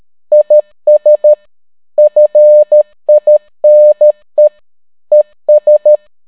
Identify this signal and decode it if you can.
Morse code, heard as a keyed tone, about 13 wpm, 600 Hz. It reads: IS FINE ES